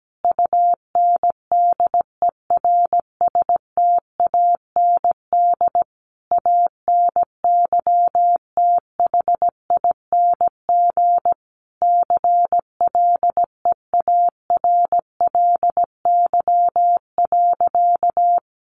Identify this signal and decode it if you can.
Morse code, heard as a keyed tone, about 17 wpm, 700 Hz. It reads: UNDERSTAND ANYTHING CLEARLY.
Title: Sentences from Top 1300 Words 17wpm